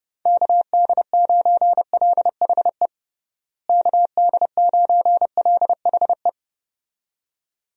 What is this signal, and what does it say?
Morse code, heard as a keyed tone, about 32 wpm, 700 Hz. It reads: KD9LHE KD9LHE
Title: USA Call Signs - 6 Characters - Repeated 2x - 30wpm